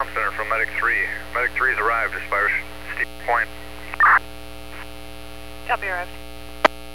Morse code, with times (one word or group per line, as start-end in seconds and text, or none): none